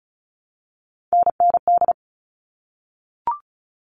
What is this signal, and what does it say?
Morse code, heard as a keyed tone, about 34 wpm, 700 Hz. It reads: NND
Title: All Cut Numbers - 3-Digit 35wpm